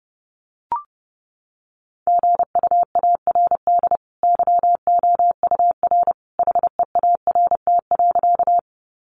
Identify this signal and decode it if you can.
Morse code, heard as a keyed tone, about 28 wpm, 700 Hz. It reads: GUARD YOUR HEART.
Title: Sentences from Top 1200 Words 30wpm